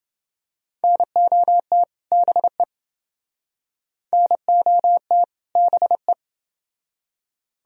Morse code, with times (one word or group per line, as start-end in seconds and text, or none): 0.8-1.8 NOT
2.1-2.6 BE
4.1-5.2 NOT
5.6-6.1 BE